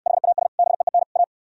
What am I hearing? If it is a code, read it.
Morse code, heard as a keyed tone, about 68 wpm, 700 Hz. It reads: HAS BEEN A